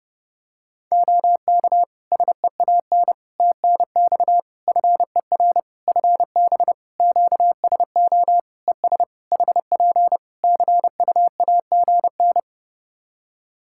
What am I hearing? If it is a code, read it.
Morse code, heard as a keyed tone, about 31 wpm, 700 Hz. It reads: OK SEAN TNX FER FB QSO ES HP CUAGN
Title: Practice QSO - Exchange 4 - Ending - 30wpm